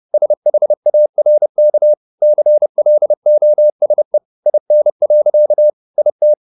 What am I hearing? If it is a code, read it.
Morse code, heard as a keyed tone, about 30 wpm, 600 Hz. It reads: SHARK CLOSE IN. IT